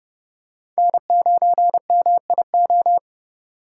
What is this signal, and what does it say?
Morse code, heard as a keyed tone, about 31 wpm, 700 Hz. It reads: N9MIO